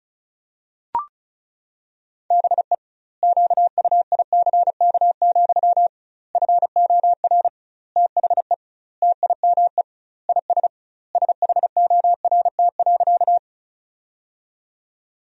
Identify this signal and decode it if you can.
Morse code, heard as a keyed tone, about 34 wpm, 700 Hz. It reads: BE QUICK, FOR THE TIME IS SHORT.
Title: 2x Word-Spacing - Sentences from Top 500 Words - 35wpm